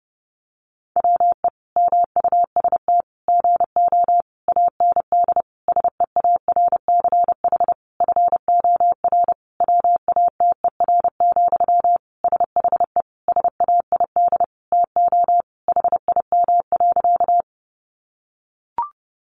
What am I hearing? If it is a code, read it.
Morse code, heard as a keyed tone, about 30 wpm, 700 Hz. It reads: WE MUST GO AND SEARCH FOR WATER, SHE SAID TO HIM.
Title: Sentences from Top 800 Words 30wpm